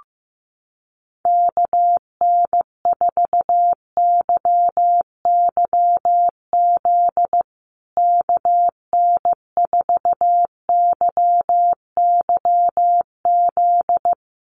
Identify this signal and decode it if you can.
Morse code, heard as a keyed tone, about 15 wpm, 700 Hz. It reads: KN4YYZ KN4YYZ
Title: USA Call Signs Repeated 3x 15wpm